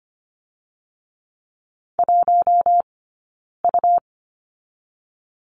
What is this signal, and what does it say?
Morse code, heard as a keyed tone, about 25 wpm, 700 Hz. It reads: 1 U
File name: course / 25:10wpm / 099 Lesson 17d - Letter u - Two Characters 25wpm